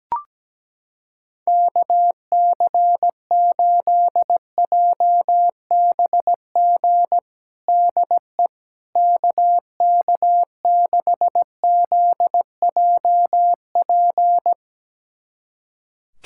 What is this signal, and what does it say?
Morse code, heard as a keyed tone, about 17 wpm, 700 Hz. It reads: KC8JBG DE KK6ZJP
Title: USA Callsign DE Callsign - 6 Characters - 17wpm